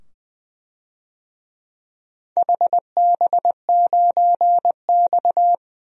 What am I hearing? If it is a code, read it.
Morse code, heard as a keyed tone, about 20 wpm, 700 Hz. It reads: HB9X
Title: DXCC-100 Call Signs - 4 Characters - 20wpm